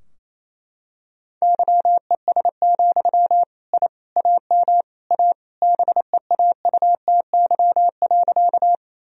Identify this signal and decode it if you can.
Morse code, heard as a keyed tone, about 27 wpm, 700 Hz. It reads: YES, I AM A BEAUTY.
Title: Sentences from Top 1000 Words - Review Part 2 - 28wpm